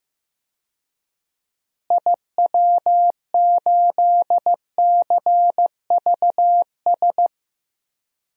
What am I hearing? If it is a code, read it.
Morse code, heard as a keyed tone, about 15 wpm, 700 Hz. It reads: IW8CVS
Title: DXCC-100 Call Signs - 6 Characters - Repeated 2x - 15wpm